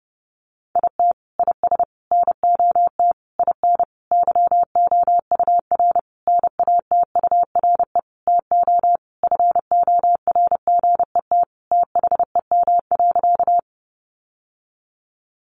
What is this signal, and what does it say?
Morse code, heard as a keyed tone, about 30 wpm, 700 Hz. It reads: IT IS NOT IN YOUR NATURE TO FORGET THEM.
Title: Sentences from Top 1200 Words 30wpm